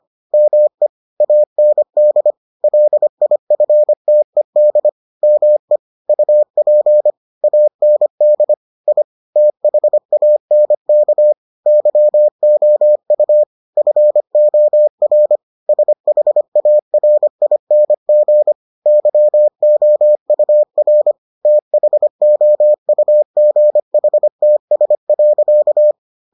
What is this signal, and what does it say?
Morse code, heard as a keyed tone, about 25 wpm, 600 Hz. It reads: ME AND LIFTED ME UP AND I THANK YOU FOR SHARING YOUR THOUGHTS.